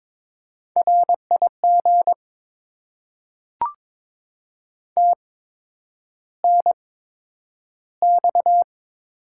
Morse code, heard as a keyed tone, about 22 wpm, spaced as for 9 wpm, 700 Hz. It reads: RIG TNX
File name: Top 20 QSO Elements - 8x Character Spacing - 22wpm